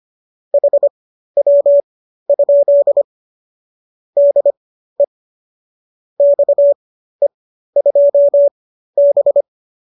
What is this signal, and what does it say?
Morse code, heard as a keyed tone, about 25 wpm, spaced as for 13 wpm, 570 Hz. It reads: HW? DE XE2B